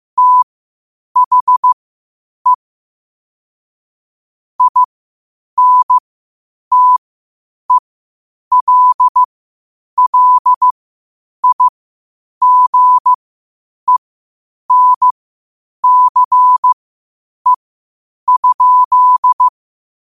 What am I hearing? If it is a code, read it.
Morse code, heard as a keyed tone, about 15 wpm, spaced as for 8 wpm, 1000 Hz. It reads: THE INTELLIGENCE?
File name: qotdfiles / 8wpm